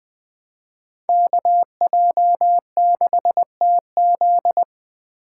Morse code, heard as a keyed tone, about 24 wpm, 700 Hz. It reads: KJ6TZ
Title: Speed Racing - USA 2x2 Call Signs 20wpm